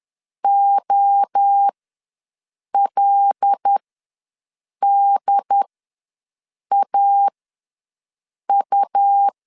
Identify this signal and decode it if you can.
Morse code, heard as a keyed tone, about 10 wpm, spaced as for 6 wpm, 800 Hz. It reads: OLDAU
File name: LEZIONI / 5 LBVC / 5 L ws 10wpm 3636